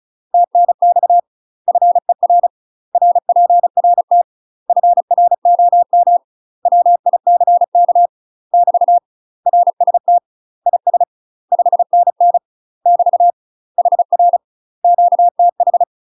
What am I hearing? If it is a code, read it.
Morse code, heard as a keyed tone, about 36 wpm, 700 Hz. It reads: TNX FER RPRT FROM WICK <BT> RST IS 5NN <BT> HR QTH
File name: UK MASTER QSOs TRAINING / 35WPM / UK MASTER QSOs 0026